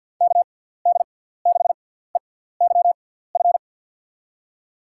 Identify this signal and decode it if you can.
Morse code, heard as a keyed tone, about 52 wpm, spaced as for 17 wpm, 700 Hz. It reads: KD6EYF
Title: USA Call Signs - 6x Character Spacing - 50wpm